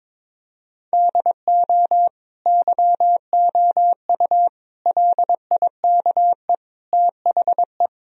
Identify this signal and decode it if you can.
Morse code, heard as a keyed tone, about 22 wpm, 700 Hz. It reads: DO YOU LIKE THE
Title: Sentences from Top 1000 Words - Review Part 2 - 22wpm